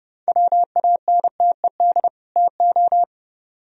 Morse code, heard as a keyed tone, about 37 wpm, 700 Hz. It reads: WANTED TO
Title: Speed-Racing - Sets of 2 Words 30wpm